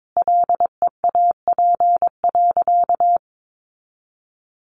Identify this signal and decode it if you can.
Morse code, heard as a keyed tone, about 22 wpm, 700 Hz. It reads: LEAP.
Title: A Sound Of Thunder 22wpm